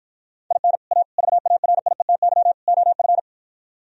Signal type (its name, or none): Morse code